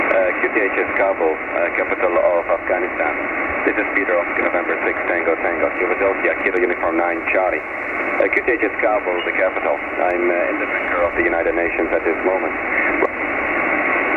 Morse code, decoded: none